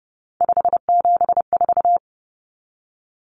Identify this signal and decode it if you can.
Morse code, heard as a keyed tone, about 30 wpm, 700 Hz. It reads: 574
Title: Common Cut Numbers - 3-Digit 30wpm